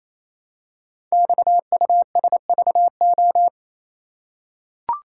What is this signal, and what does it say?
Morse code, heard as a keyed tone, about 28 wpm, 700 Hz. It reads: XUSVO
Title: Code Groups - 5 Letter:Numbers - 28wpm